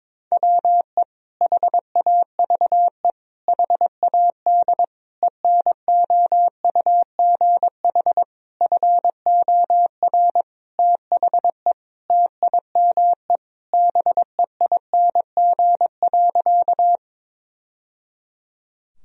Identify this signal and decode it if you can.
Morse code, heard as a keyed tone, about 21 wpm, 700 Hz. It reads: WE HAVE HAD ENOUGH FOR THE TIME BEING.
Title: Sentences from Top 1000 Words - Review Part 3 - 22wpm